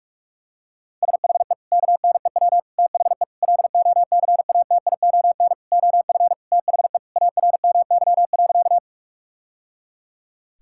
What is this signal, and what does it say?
Morse code, heard as a keyed tone, about 46 wpm, 700 Hz. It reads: SHE KNEW THE LOCATION OF THE ARMY.
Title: Sentences from Top 1000 Words - Review Part 6 - 45wpm